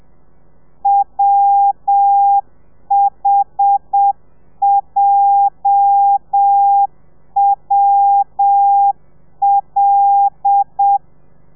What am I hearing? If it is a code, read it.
Morse code, heard as a keyed tone, about 7 wpm, 800 Hz. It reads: WHJWL
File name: telegrafia / i2tfj / Lez8 35